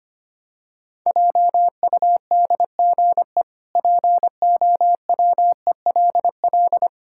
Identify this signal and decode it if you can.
Morse code, heard as a keyed tone, about 25 wpm, 700 Hz. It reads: JUDGE POWELL
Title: Pangram Challenge v2 - 25wpm